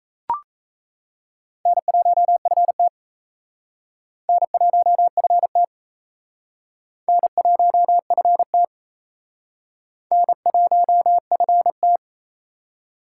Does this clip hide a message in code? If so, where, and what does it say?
Yes: Morse code from 1.7 to 12.0 s, heard as a keyed tone, about 35 wpm, 700 Hz. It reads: N1FT N1FT N1FT N1FT